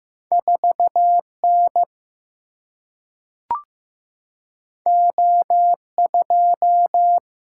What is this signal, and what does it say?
Morse code, heard as a keyed tone, about 15 wpm, 700 Hz. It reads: 4N O2
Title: Single-Letter Single-Number 15wpm